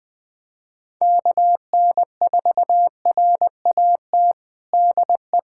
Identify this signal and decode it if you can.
Morse code, heard as a keyed tone, about 20 wpm, 700 Hz. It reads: KN4RAT DE
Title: USA Callsign DE Callsign - 6 Characters - 20wpm